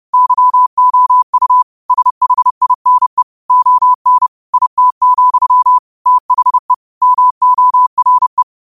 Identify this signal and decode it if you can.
Morse code, heard as a keyed tone, about 30 wpm, 1000 Hz. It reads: YOU SHINE ON IT, THE MORE